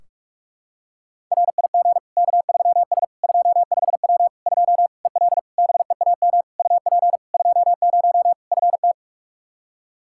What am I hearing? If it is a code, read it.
Morse code, heard as a keyed tone, about 46 wpm, 700 Hz. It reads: RIG K3S 25W 2 EL BEAM UP 20 FT
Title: Practice QSO - Exchange 2 - Info - 45wpm